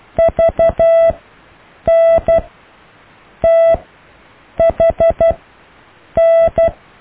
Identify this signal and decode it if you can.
Morse code, heard as a keyed tone, about 12 wpm, spaced as for 7 wpm, 660 Hz. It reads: VNTHN
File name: cw / EKTPV+BFHNW